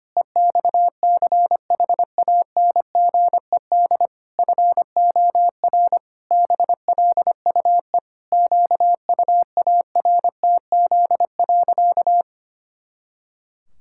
Morse code, heard as a keyed tone, about 25 wpm, 700 Hz. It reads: EXCHANGED FOR BLUE QUARTZ.
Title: Pangram Challenge v2 - 25wpm